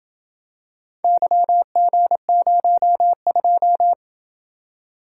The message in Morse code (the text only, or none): YG02